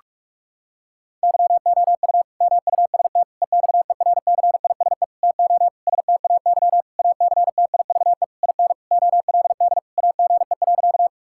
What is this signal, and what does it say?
Morse code, heard as a keyed tone, about 46 wpm, 700 Hz. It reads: YOU MUST EXERCISE TO STAY ACTIVE IN OLD AGE.
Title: Sentences from Top 1000 Words - Review Part 6 - 45wpm